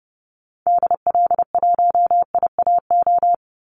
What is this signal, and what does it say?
Morse code, heard as a keyed tone, about 30 wpm, 700 Hz. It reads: DL1IAO
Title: Call Signs Repeated 2x - 30wpm